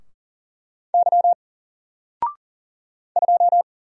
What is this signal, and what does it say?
Morse code, heard as a keyed tone, about 40 wpm, 700 Hz. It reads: Y 2
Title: Single Character (Letter:Number:Punctuation:Prosign) 40wpm